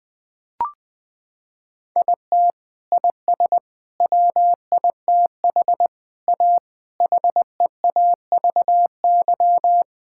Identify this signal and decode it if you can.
Morse code, heard as a keyed tone, about 20 wpm, 700 Hz. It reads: IT IS WITH A HEAVY
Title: Sets of 8 Words - 20wpm